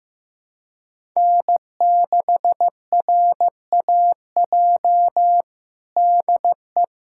Morse code, heard as a keyed tone, about 15 wpm, 700 Hz. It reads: N6RAJ DE
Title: USA Callsign DE Callsign - 5 Characters - 15wpm